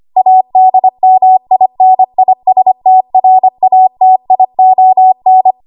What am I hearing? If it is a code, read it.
Morse code, heard as a keyed tone, about 25 wpm, 750 Hz. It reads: ADMINISTRATION
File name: Archive / 25 WPM / 25WPM